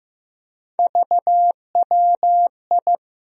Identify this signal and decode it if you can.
Morse code, heard as a keyed tone, about 15 wpm, 700 Hz. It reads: VWI
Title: USA Call Sign Postfixes - 3 Letters 15wpm